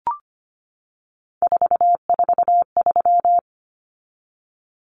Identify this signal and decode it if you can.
Morse code, heard as a keyed tone, about 25 wpm, 700 Hz. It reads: 443
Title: Numbers - 3-Digit 25wpm